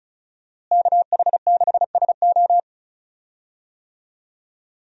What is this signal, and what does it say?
Morse code, heard as a keyed tone, about 36 wpm, 700 Hz. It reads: KH6SO
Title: USA Call Signs - 5 Characters - Repeated 3x - 35wpm